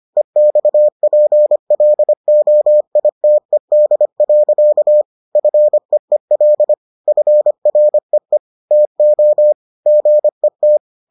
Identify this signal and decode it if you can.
Morse code, heard as a keyed tone, about 25 wpm, 600 Hz. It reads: EXPLOITED. FEEL FREE TO GET